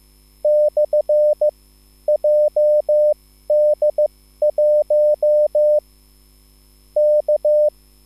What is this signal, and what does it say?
Morse code, heard as a keyed tone, about 15 wpm, spaced as for 11 wpm, 600 Hz. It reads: /JD1 K
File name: QSL / CW lesson